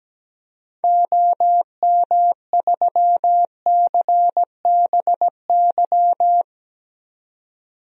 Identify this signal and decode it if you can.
Morse code, heard as a keyed tone, about 17 wpm, 700 Hz. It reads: OM3CBY
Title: DXCC-100 Call Signs - 6 Characters - 17wpm